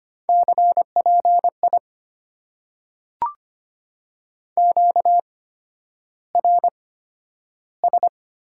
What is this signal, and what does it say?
Morse code, heard as a keyed tone, about 25 wpm, spaced as for 11 wpm, 690 Hz. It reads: CPI QRS
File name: Top 90 QSO Elements - 8x Character Spacing - 25wpm